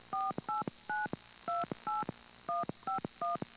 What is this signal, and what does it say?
DTMF, heard as telephone touch tones, tones of about 170 ms. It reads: 48928151